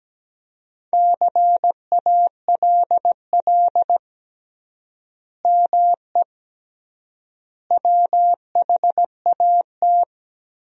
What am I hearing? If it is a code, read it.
Morse code, heard as a keyed tone, about 17 wpm, 700 Hz. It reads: CALL ME WHAT
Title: 3x Word-Spacing - Sentences from Top 200 Words - 17wpm